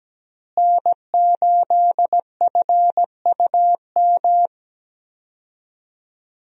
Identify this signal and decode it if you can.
Morse code, heard as a keyed tone, about 17 wpm, 700 Hz. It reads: N8FUM
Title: USA Call Signs - 5 Characters - Repeated 2x - 17wpm